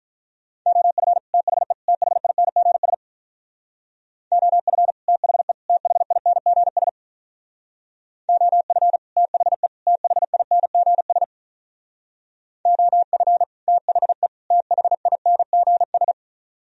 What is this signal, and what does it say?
Morse code, heard as a keyed tone, about 43 wpm, 700 Hz. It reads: OF THE THINGS OF THE THINGS OF THE THINGS OF THE THINGS
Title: Speed-Racing - Sets of 3 Words 35wpm